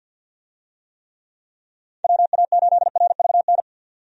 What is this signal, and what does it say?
Morse code, heard as a keyed tone, about 52 wpm, 700 Hz. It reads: WA8RVN
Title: USA Call Signs - 6 Characters - 50wpm